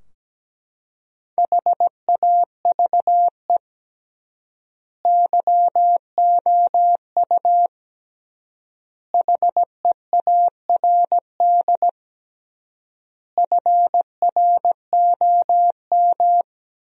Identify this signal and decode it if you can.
Morse code, heard as a keyed tone, about 17 wpm, 700 Hz. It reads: HAVE YOU HEARD FROM
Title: 3x Word-Spacing - Sentences from Top 400 Words - 17wpm